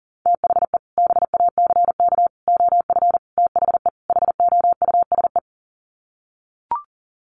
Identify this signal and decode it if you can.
Morse code, heard as a keyed tone, about 40 wpm, 700 Hz. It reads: THE BACK OF THE HOUSE
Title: Sets of 5 Words Encore2 40wpm